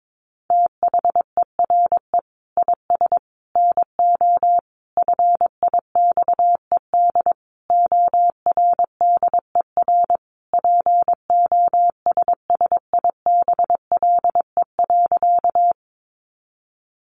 Morse code, heard as a keyed tone, about 22 wpm, 700 Hz. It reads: THERE IS NO FIXED ORDER POSSIBLE.
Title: Sentences from Top 1100 Words 22wpm